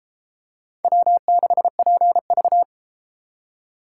Morse code, heard as a keyed tone, about 32 wpm, 700 Hz. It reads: W6PV